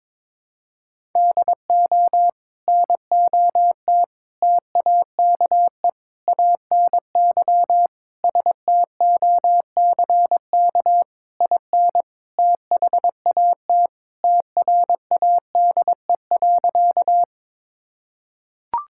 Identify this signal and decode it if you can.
Morse code, heard as a keyed tone, about 22 wpm, 700 Hz. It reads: DO NOT TAKE ANY STOCK IN THAT TRADE.
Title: Sentences from Top 1000 Words - Review Part 5 - 22wpm